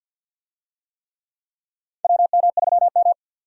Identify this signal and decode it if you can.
Morse code, heard as a keyed tone, about 50 wpm, 700 Hz. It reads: WM3M
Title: USA Call Signs - 4 Characters - 50wpm